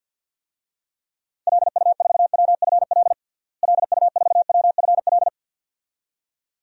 Morse code, heard as a keyed tone, about 52 wpm, 700 Hz. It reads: LU4WFL LU4WFL
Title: DXCC-100 Call Signs - 6 Characters - Repeated 2x - 50wpm